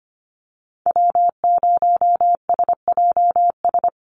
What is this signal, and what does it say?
Morse code, heard as a keyed tone, about 25 wpm, 700 Hz. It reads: W0SJS
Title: Call Signs Repeated 2x - 25wpm